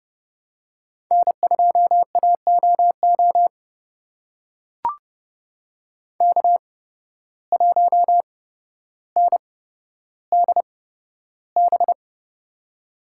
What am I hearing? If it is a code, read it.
Morse code, heard as a keyed tone, about 30 wpm, spaced as for 13 wpm, 700 Hz. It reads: N2AOO K1NDB
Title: USA Call Signs - 5 Characters - 8x Character Spacing - 30wpm